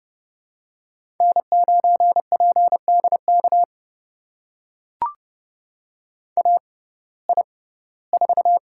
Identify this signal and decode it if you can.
Morse code, heard as a keyed tone, about 30 wpm, spaced as for 18 wpm, 700 Hz. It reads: N9PDK AI4
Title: USA Call Signs - 5 Characters - 6x Character Spacing - 30wpm